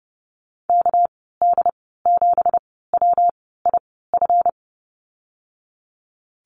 Morse code, heard as a keyed tone, about 30 wpm, spaced as for 17 wpm, 700 Hz. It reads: KD7WIF
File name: Lesson 37g - Letter x - Call Signs 30wpm